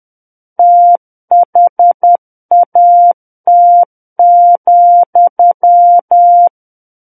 Morse code, heard as a keyed tone, about 10 wpm, 700 Hz. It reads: THAT,
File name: Book One / 01-Chapter 1 - 10 WPM